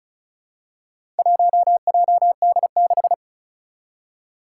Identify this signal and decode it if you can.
Morse code, heard as a keyed tone, about 36 wpm, 700 Hz. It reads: 1JD6
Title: Four Challenging Letter-Number Combinations 35wpm